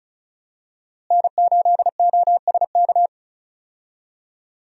Morse code, heard as a keyed tone, about 34 wpm, 700 Hz. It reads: N8OSK
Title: USA Call Signs - 5 Characters - Repeated 2x - 35wpm